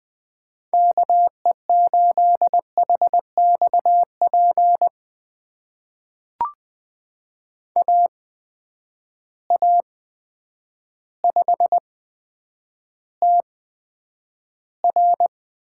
Morse code, heard as a keyed tone, about 20 wpm, spaced as for 9 wpm, 700 Hz. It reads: KE8HXP AA5TR
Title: USA Call Signs - 8x Character Spacing - 20wpm